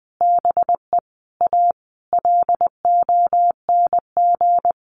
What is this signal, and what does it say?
Morse code, heard as a keyed tone, about 20 wpm, 700 Hz. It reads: BE A LONG
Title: Sentences from Top 1100 Words 20wpm